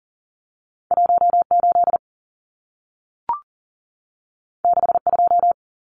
Morse code, heard as a keyed tone, about 40 wpm, 700 Hz. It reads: 18 62
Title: Numbers - 2-Digit 40wpm